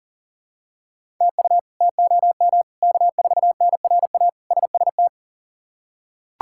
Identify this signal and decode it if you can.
Morse code, heard as a keyed tone, about 40 wpm, 700 Hz. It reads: TU TOM K4NRA SST
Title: K1USN SST QSO Practice - 40wpm